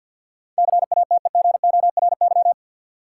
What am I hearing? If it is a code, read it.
Morse code, heard as a keyed tone, about 52 wpm, 700 Hz. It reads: CATEGORY
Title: Sentences from Top 1000 Words - Review Part 7 - New Word Drill - 50wpm